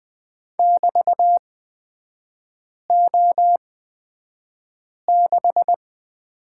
Morse code, heard as a keyed tone, about 20 wpm, spaced as for 5 wpm, 700 Hz. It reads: <BT>O6
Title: 224 Lesson 36e - BT - Three Characters 20wpm